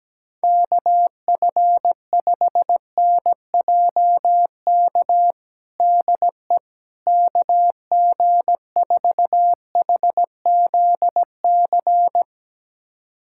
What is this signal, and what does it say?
Morse code, heard as a keyed tone, about 17 wpm, 700 Hz. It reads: KF5NJK DE KG4HZC
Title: USA Callsign DE Callsign - 6 Characters - 17wpm